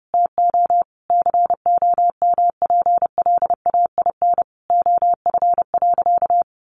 Morse code, heard as a keyed tone, about 30 wpm, 700 Hz. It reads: TO COMPLAIN OF.
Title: Sentences from Top 1300 Words 30wpm